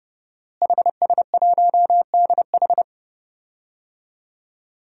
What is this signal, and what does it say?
Morse code, heard as a keyed tone, about 30 wpm, 700 Hz. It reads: HS1DH
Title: DXCC-100 Call Signs - 5 Characters - Repeated 2x - 30wpm